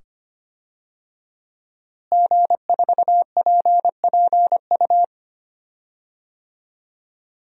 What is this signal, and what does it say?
Morse code, heard as a keyed tone, about 25 wpm, 700 Hz. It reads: G4PPU
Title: DXCC-100 Call Signs - 5 Characters - Repeated 2x - 25wpm